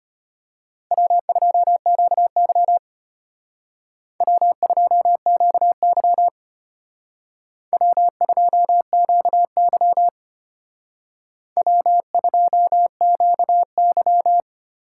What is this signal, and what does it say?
Morse code, heard as a keyed tone, about 29 wpm, 700 Hz. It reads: W2QY W2QY W2QY W2QY
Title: Speed Racing - USA 1x2 Call Signs 25wpm